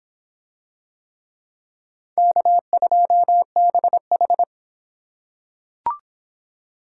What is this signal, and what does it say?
Morse code, heard as a keyed tone, about 23 wpm, 700 Hz. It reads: K2BH